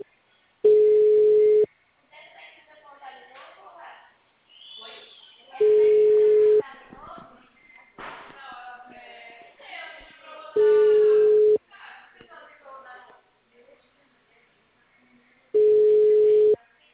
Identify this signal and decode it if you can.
Ringback tone (Europe, 425 Hz), with speech.